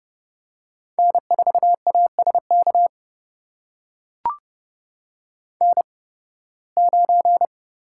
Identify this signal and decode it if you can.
Morse code, heard as a keyed tone, about 30 wpm, spaced as for 20 wpm, 700 Hz. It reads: N4ASK N9
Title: USA Call Signs - 5 Characters - 8x Character Spacing - 30wpm